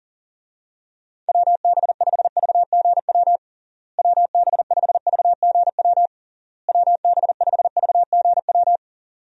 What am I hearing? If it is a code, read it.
Morse code, heard as a keyed tone, about 40 wpm, 700 Hz. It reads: WB5VGW WB5VGW WB5VGW
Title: USA Call Signs - 6 Characters - Repeated 3x - 40wpm